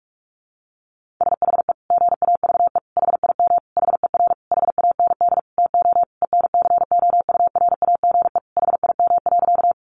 Morse code, heard as a keyed tone, about 44 wpm, 700 Hz. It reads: SHE GAVE HIM HER HAND TO ENCOURAGE HIM.
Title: Sentences from Top 1400 Words 45wpm